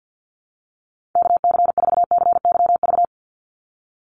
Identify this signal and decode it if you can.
Morse code, heard as a keyed tone, about 50 wpm, 700 Hz. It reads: KC4CYV
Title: USA Call Signs 50wpm